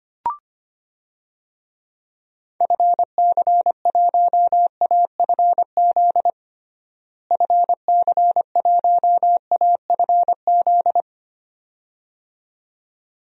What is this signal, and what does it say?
Morse code, heard as a keyed tone, about 25 wpm, 700 Hz. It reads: FC1AFZ FC1AFZ
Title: DXCC-100 Call Signs - 6 Characters - Repeated 2x - 25wpm